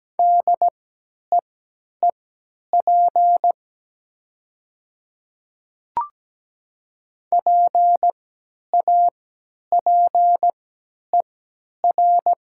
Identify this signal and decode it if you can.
Morse code, heard as a keyed tone, about 17 wpm, spaced as for 10 wpm, 700 Hz. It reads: DEEP PAPER
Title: 3x Character Spacing - Top 500 Words - 17wpm